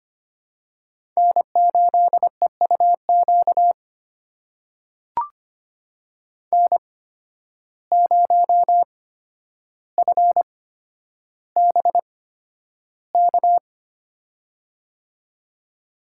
Morse code, heard as a keyed tone, about 25 wpm, spaced as for 11 wpm, 700 Hz. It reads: N8EUQ N0FBK
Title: USA Call Signs - 5 Characters - 8x Character Spacing - 25wpm